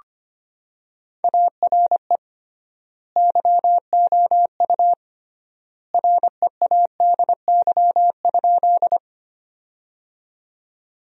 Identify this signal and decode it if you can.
Morse code, heard as a keyed tone, about 25 wpm, 700 Hz. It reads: ARE YOU READY?
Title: 3x Word-Spacing - Sentences from Top 400 Words - 25wpm